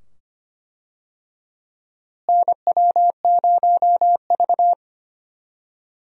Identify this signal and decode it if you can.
Morse code, heard as a keyed tone, about 25 wpm, 700 Hz. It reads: NW0V
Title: USA Call Signs - 4 Characters - Repeated 2x - 25wpm